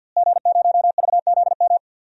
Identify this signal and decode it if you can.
Morse code, heard as a keyed tone, about 49 wpm, 700 Hz. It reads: G0VZM
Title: DXCC-100 Call Signs - 5 Characters - Repeated 3x - 50wpm